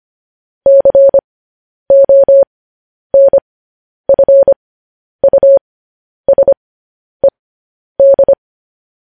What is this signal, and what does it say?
Morse code, heard as a keyed tone, about 25 wpm, spaced as for 10 wpm, 550 Hz. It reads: CONFUSED
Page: news-headlines-in-morse-code-at-10-wpm